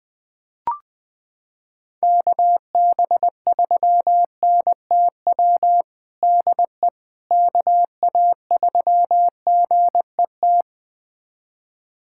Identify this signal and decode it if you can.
Morse code, heard as a keyed tone, about 20 wpm, 700 Hz. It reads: KB3NTW DE KA3GET
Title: USA Callsign DE Callsign - 6 Characters - 20wpm